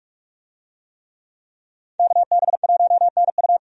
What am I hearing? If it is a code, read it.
Morse code, heard as a keyed tone, about 44 wpm, 700 Hz. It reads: KB1NU
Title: USA Call Signs - 5 Characters - 45wpm